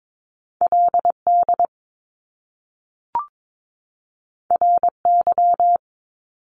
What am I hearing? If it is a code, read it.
Morse code, heard as a keyed tone, about 22 wpm, 700 Hz. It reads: LD RY
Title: Top Two Letter Combinations in Words 22wpm